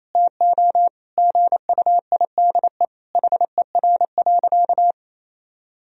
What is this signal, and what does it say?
Morse code, heard as a keyed tone, about 28 wpm, 700 Hz. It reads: TO GUIDE HER.
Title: Sentences from Top 1000 Words - Review Part 4 - 28wpm